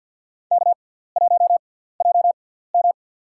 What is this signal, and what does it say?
Morse code, heard as a keyed tone, about 52 wpm, spaced as for 17 wpm, 700 Hz. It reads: K1JM
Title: USA Call Signs - 6x Character Spacing - 50wpm